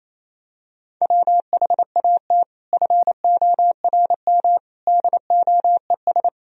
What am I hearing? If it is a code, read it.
Morse code, heard as a keyed tone, about 28 wpm, 700 Hz. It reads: WHAT FORM DOES
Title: Sentences from Top 1000 Words - Review Part 1 - 28wpm